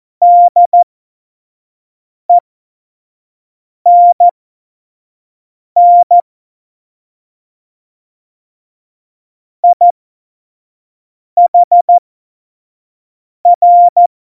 Morse code, heard as a keyed tone, about 14 wpm, spaced as for 5 wpm, 700 Hz. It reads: DENN IHR
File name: Morsekurs / Witze / Witz 03.Juli 2018, 25 F70